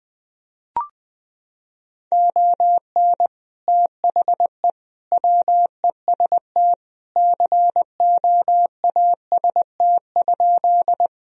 Morse code, heard as a keyed tone, about 20 wpm, 700 Hz. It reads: ON THE WEST COAST?
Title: Sentences from Top 1000 Words - Review Part 4 - 20wpm